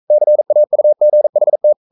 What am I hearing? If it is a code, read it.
Morse code, heard as a keyed tone, about 42 wpm, 600 Hz. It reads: CAUGHT
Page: Romanen Never af Ken Follet på engelsk.